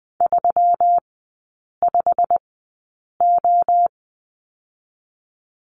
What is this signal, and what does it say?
Morse code, heard as a keyed tone, about 20 wpm, spaced as for 8 wpm, 700 Hz. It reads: 35O
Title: 121 Lesson 20e - Number 3 - Three Characters 20wpm